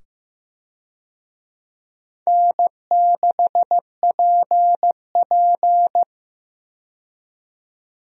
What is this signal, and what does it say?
Morse code, heard as a keyed tone, about 15 wpm, 700 Hz. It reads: N6PP
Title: USA Call Signs - 4 Characters - 15wpm